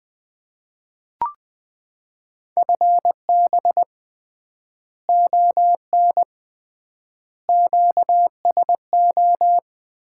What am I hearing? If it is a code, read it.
Morse code, heard as a keyed tone, about 20 wpm, 700 Hz. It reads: FB ON QSO
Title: Common QSO Phrases - 3x Word-Spacing - 20wpm